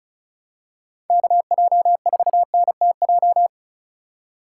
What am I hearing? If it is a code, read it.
Morse code, heard as a keyed tone, about 34 wpm, 700 Hz. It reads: KJ4NTJ
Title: USA Call Signs - 6 Characters - 2x Character Spacing - 35wpm